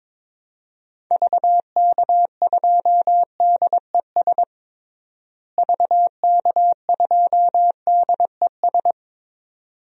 Morse code, heard as a keyed tone, about 22 wpm, 700 Hz. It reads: VK2DES VK2DES
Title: DXCC-100 Call Signs - 6 Characters - Repeated 2x - 22wpm